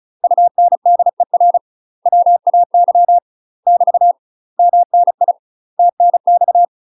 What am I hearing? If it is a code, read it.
Morse code, heard as a keyed tone, about 35 wpm, 700 Hz. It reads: UNDER WAY <BT> MNI TNX
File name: UK MASTER QSOs TRAINING/35WPM